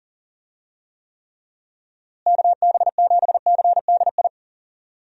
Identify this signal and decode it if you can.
Morse code, heard as a keyed tone, about 37 wpm, 700 Hz. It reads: KB7CDI
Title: USA Call Signs - 6 Characters - Repeated 3x - 40wpm